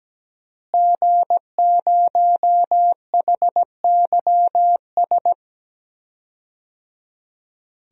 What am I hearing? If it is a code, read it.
Morse code, heard as a keyed tone, about 17 wpm, 700 Hz. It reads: G0HYS
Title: DXCC-100 Call Signs - 5 Characters - Repeated 3x - 17wpm